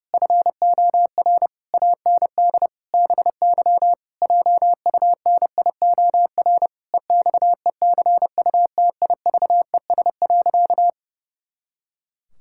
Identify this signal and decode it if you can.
Morse code, heard as a keyed tone, about 30 wpm, 700 Hz. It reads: FOR AND BY JUNIOR EXECUTIVES.
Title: Pangram Challenge v2 - 30wpm